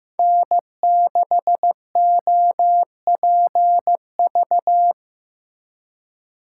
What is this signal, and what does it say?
Morse code, heard as a keyed tone, about 18 wpm, 700 Hz. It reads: N6OPV
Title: Speed Racing - USA 1x3 Call Signs 15wpm